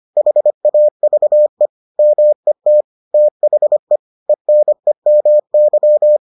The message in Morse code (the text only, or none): HAVE MET THE ENEMY